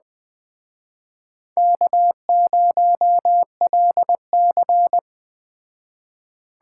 Morse code, heard as a keyed tone, about 19 wpm, 700 Hz. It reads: K0LC
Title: Speed Racing - USA 1x2 Call Signs 17wpm